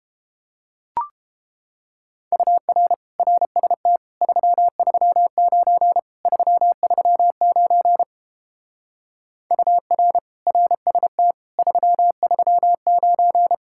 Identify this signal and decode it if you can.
Morse code, heard as a keyed tone, about 29 wpm, 700 Hz. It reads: UR RST 339 339 UR RST 339